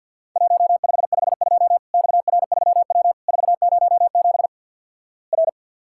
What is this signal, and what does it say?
Morse code, heard as a keyed tone, about 50 wpm, 700 Hz. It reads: 1552 XR3W 407 R
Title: Worked All Europe DX Contest - QTC Exchange - 50wpm